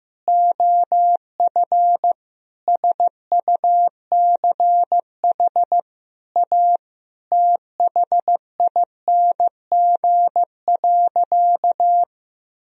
Morse code, heard as a keyed tone, about 15 wpm, 700 Hz. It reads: OF SUCH A THING.
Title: Sentences from Top 1000 Words - Review Part 1 - 15wpm